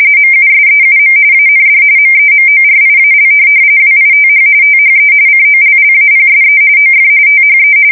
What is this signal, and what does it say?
RTTY, about 45.45 baud. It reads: MUSEUM AND MEMORIAL WITH THE COLLAPSE OF THE
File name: fsk170 / battleshipnj